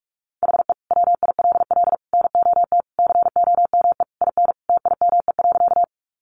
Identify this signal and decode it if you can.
Morse code, heard as a keyed tone, about 44 wpm, 700 Hz. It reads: HE WILL NOT COME IN TIME.
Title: Sentences from Top 600 Words 45wpm